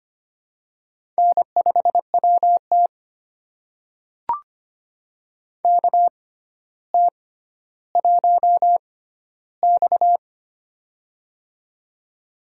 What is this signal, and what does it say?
Morse code, heard as a keyed tone, about 25 wpm, spaced as for 13 wpm, 700 Hz. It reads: N5WT KT1X
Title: USA Call Signs - 4 Characters - 6x Character Spacing - 25wpm